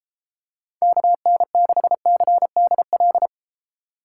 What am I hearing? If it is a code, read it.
Morse code, heard as a keyed tone, about 32 wpm, 700 Hz. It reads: KN6CDL